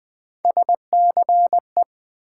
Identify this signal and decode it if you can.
Morse code, heard as a keyed tone, about 21 wpm, 700 Hz. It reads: SCE
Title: Build A Word - 1 Letter At A Time - 5-Letter Words - 20wpm